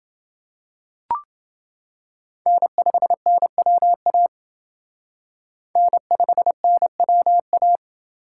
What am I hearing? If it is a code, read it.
Morse code, heard as a keyed tone, about 28 wpm, 700 Hz. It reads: N5NWA N5NWA